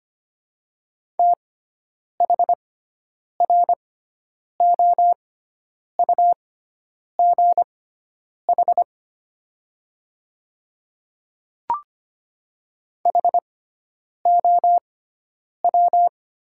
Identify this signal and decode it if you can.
Morse code, heard as a keyed tone, about 25 wpm, spaced as for 9 wpm, 690 Hz. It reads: THROUGH HOW
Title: 6x Character Spacing - Top 200 Words - 25wpm